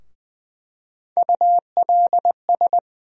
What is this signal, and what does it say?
Morse code, heard as a keyed tone, about 20 wpm, 700 Hz. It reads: ULS
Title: USA Call Sign Postfixes - 3 Letters 20wpm